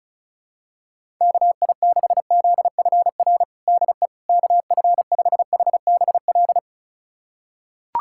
Morse code, heard as a keyed tone, about 36 wpm, 700 Hz. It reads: KI6ZFR DE KF5HBL
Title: USA Callsign DE Callsign - 6 Characters - 35wpm